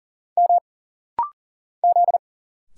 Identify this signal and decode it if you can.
Morse code, heard as a keyed tone, about 40 wpm, 700 Hz. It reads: M Z
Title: Single Letter-Number Alt - Rapid-Fire 40wpm